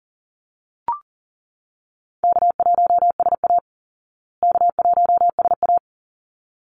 Morse code, heard as a keyed tone, about 40 wpm, 700 Hz. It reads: K1SA K1SA
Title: Easy Call Signs Repeated 2x - 40wpm